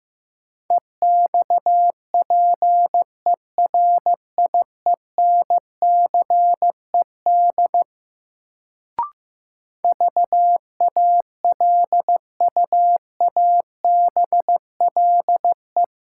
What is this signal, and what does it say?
Morse code, heard as a keyed tone, about 15 wpm, 700 Hz. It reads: EXPERIENCED VALUABLE
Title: Sentences from Top 1100 Words - New Word Drill - 15wpm